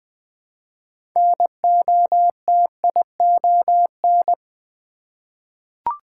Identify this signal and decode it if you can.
Morse code, heard as a keyed tone, about 20 wpm, 700 Hz. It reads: NOTION
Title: 6 Letter Words - First 3 Letters Given - 20wpm